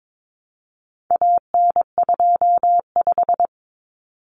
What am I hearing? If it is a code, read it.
Morse code, heard as a keyed tone, about 22 wpm, 700 Hz. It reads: AN25
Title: Common Cut Numbers - 4-Digit 22wpm